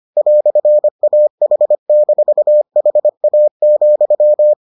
Morse code, heard as a keyed tone, about 25 wpm, 600 Hz. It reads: "AH-HA,